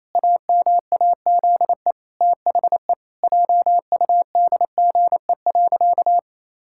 Morse code, heard as a keyed tone, about 28 wpm, 700 Hz. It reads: AMAZE THE JUDGE.
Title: Pangram Challenge v2 - 28wpm